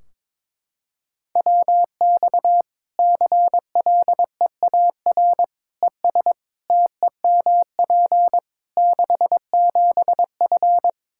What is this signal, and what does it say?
Morse code, heard as a keyed tone, about 22 wpm, 700 Hz. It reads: WX CLEAR ES TEMP 67F
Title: K1USN SST QSO Practice 22wpm